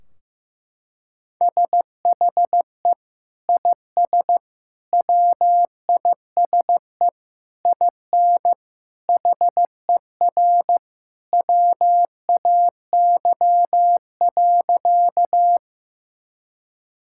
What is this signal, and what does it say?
Morse code, heard as a keyed tone, about 15 wpm, 700 Hz. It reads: SHE IS WISE IN HER WAY.
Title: Sentences from Top 1600 Words 15wpm